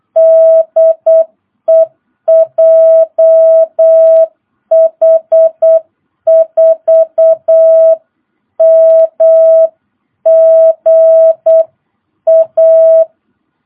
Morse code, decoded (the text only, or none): DE JH4MGA